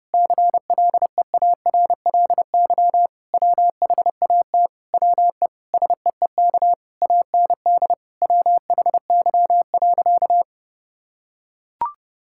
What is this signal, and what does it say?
Morse code, heard as a keyed tone, about 30 wpm, 700 Hz. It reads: CLEARLY WHAT WE SEEK AND WHY.
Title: Sentences from Top 1600 Words 30wpm